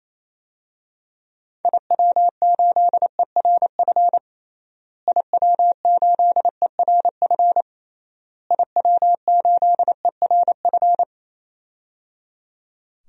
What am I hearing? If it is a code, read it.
Morse code, heard as a keyed tone, about 28 wpm, 700 Hz. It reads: IW8ERF IW8ERF IW8ERF
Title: DXCC-100 Call Signs - 6 Characters - Repeated 3x - 28wpm